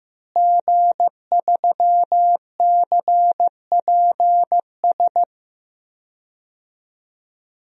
Morse code, heard as a keyed tone, about 15 wpm, 700 Hz. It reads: G3CPS
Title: DXCC-100 Call Signs - 5 Characters - Repeated 2x - 15wpm